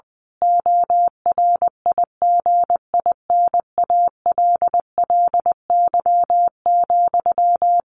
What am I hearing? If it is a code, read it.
Morse code, heard as a keyed tone, about 20 wpm, 700 Hz. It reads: ORIGINALLY,
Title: A Sound Of Thunder 20wpm